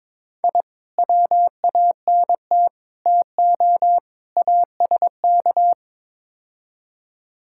Morse code, heard as a keyed tone, about 21 wpm, 700 Hz. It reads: I WANT TO ASK
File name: practice / 22wpm / Build A Phrase - 5-Word Phrases - 22wpm